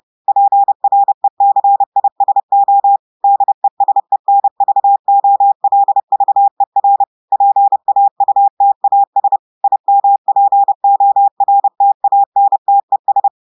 Morse code, heard as a keyed tone, about 30 wpm, 800 Hz. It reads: PRECISO DESENVOLVER PAUTAS IMPORTANTES
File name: portugais texte en clair 2 vitesse 30 mots minutes